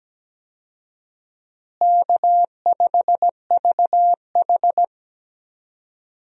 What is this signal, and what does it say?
Morse code, heard as a keyed tone, about 17 wpm, 700 Hz. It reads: K5VH
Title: USA Call Signs - 4 Characters - Repeated 3x - 17wpm